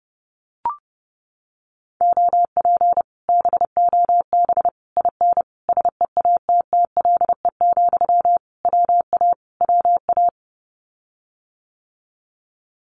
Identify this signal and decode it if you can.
Morse code, heard as a keyed tone, about 30 wpm, 700 Hz. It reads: OP BOB IN SEATTLE, WA WA
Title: Common QSO Phrases Encore 30wpm